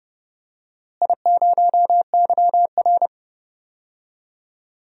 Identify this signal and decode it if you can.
Morse code, heard as a keyed tone, about 29 wpm, 700 Hz. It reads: I0YR
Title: DXCC-100 Call Signs - 4 Characters - Repeated 3x - 30wpm